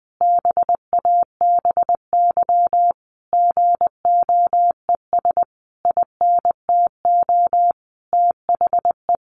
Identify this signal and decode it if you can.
Morse code, heard as a keyed tone, about 20 wpm, 700 Hz. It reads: BABY GOES INTO THE
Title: Easy Phrases 20wpm